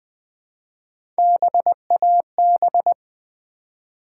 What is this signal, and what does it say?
Morse code, heard as a keyed tone, about 20 wpm, 700 Hz. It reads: BAB
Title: Build A Word - 1 Letter At A Time - 4-Letter Words - 20wpm